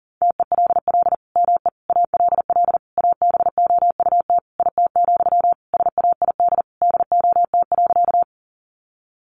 Morse code, heard as a keyed tone, about 40 wpm, 700 Hz. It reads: TELL ME ALL ABOUT IT, SAID DOT.
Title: Sentences from Top 1500 Words 40wpm